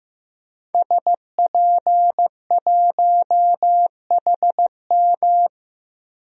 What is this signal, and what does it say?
Morse code, heard as a keyed tone, about 16 wpm, 700 Hz. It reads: SP1HM
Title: DXCC-100 Call Signs - 5 Characters - Repeated 2x - 15wpm